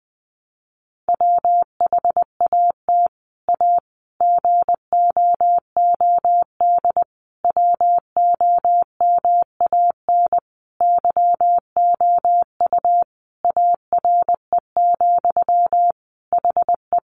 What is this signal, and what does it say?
Morse code, heard as a keyed tone, about 20 wpm, 700 Hz. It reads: WHAT A GOOD WOMAN YOU ARE, HE
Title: Sentences from Top 1100 Words 20wpm